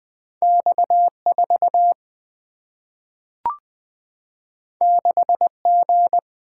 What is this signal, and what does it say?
Morse code, heard as a keyed tone, about 20 wpm, 700 Hz. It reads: X 4 6 G
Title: Single-Letter Single-Number 20wpm